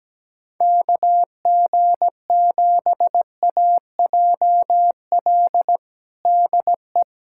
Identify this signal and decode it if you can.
Morse code, heard as a keyed tone, about 17 wpm, 700 Hz. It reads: KG7AJL DE
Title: USA Callsign DE Callsign - 6 Characters - 17wpm